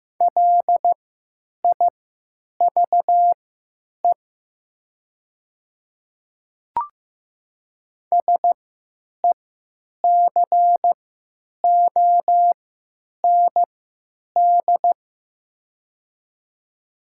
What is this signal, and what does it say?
Morse code, heard as a keyed tone, about 15 wpm, spaced as for 9 wpm, 700 Hz. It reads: LIVE SECOND
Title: 3x Character Spacing - Top 300 Words - 15wpm